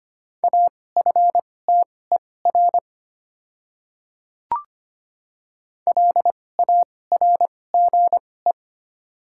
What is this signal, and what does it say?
Morse code, heard as a keyed tone, about 25 wpm, spaced as for 18 wpm, 690 Hz. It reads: AFTER LARGE
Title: 2x Character Spacing - Top 200 Words - 25wpm